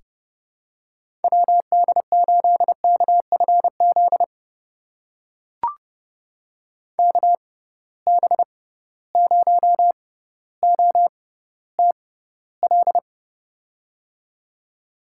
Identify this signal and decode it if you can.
Morse code, heard as a keyed tone, about 30 wpm, spaced as for 15 wpm, 700 Hz. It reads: WD8KFZ KB0OTL
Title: USA Call Signs - 6 Characters - 6x Character Spacing - 30wpm